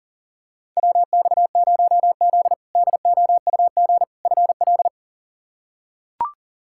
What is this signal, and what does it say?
Morse code, heard as a keyed tone, about 40 wpm, 700 Hz. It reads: WX0Z DOUG FL
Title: K1USN SST QSO Practice - 40wpm